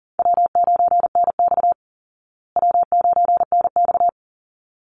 Morse code, heard as a keyed tone, about 40 wpm, 700 Hz. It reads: W9NX W9NX
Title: Easy Call Signs Repeated 2x - 40wpm